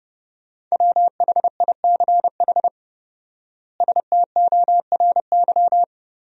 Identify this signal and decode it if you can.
Morse code, heard as a keyed tone, about 32 wpm, 700 Hz. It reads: WHICH STORY
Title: 4x Word-Spacing - Sentences from Top 300 Words - 30wpm